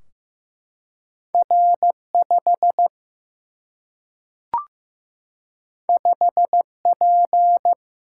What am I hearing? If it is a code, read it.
Morse code, heard as a keyed tone, about 15 wpm, 700 Hz. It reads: R5 5P
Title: Single-Letter Single-Number 15wpm